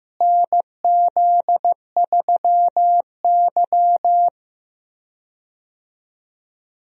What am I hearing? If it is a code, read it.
Morse code, heard as a keyed tone, about 15 wpm, 700 Hz. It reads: NZ3Y
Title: USA Call Signs - 4 Characters - Repeated 2x - 15wpm